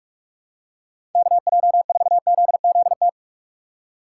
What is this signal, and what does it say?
Morse code, heard as a keyed tone, about 46 wpm, 700 Hz. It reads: KJ4ZZT
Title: USA Call Signs - 2x Character Spacing - 45wpm